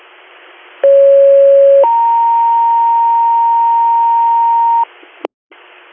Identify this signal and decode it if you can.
Two-tone page, heard as a one-second tone, then a three-second tone, about 550 then 930 Hz, with noise. It is heard from 0.8 to 4.8 s.